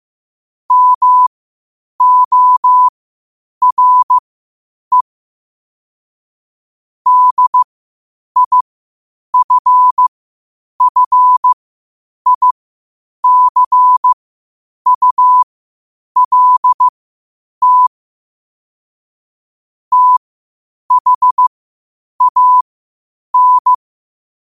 Morse code, heard as a keyed tone, about 15 wpm, spaced as for 8 wpm, 1000 Hz. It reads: MORE DIFFICULT THAN
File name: qotdfiles / 8wpm